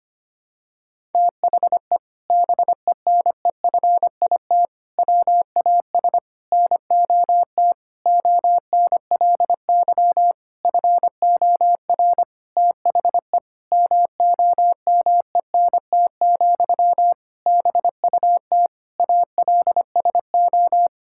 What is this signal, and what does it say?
Morse code, heard as a keyed tone, about 25 wpm, 700 Hz. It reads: THE BENEFIT WAS NOT ONLY FOR THE MOMENT, BUT ALSO
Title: Sentences from Top 1000 Words - Review Part 6 - 25wpm